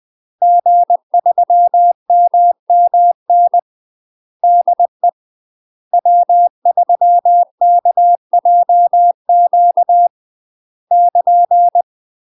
Morse code, heard as a keyed tone, about 20 wpm, 700 Hz. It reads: G3MMN DE W3KJQ <KN>
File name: UK MASTER QSOs TRAINING / 20WPM / UK MASTER QSOs 0074